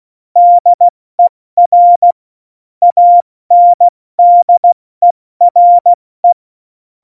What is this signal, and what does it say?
Morse code, heard as a keyed tone, about 16 wpm, 700 Hz. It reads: DER ANDERE
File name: Morsekurs / Witze / Witz 05. Januar 2021, 70 F80